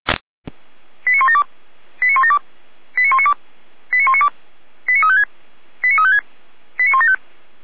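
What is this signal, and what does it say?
Five-tone selective call (ZVEI), heard as quick runs of five tones, tones of about 70 ms. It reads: 89172 89172 89192 89192 89356 89356 89175
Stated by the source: ZVEI five-tone call 89075 received by SDR.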